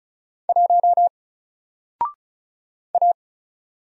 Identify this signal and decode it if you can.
Morse code, heard as a keyed tone, about 36 wpm, 700 Hz. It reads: 1 A
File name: Single-Letter-Number-35wpm